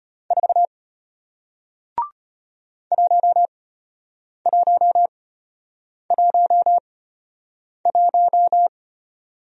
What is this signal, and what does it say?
Morse code, heard as a keyed tone, about 32 wpm, spaced as for 8 wpm, 700 Hz. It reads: 4 1111